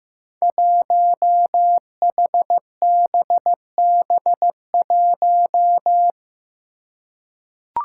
Morse code, heard as a keyed tone, about 15 wpm, 700 Hz. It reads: 1HBB1
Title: Five Challenging Letter-Number Combinations 15wpm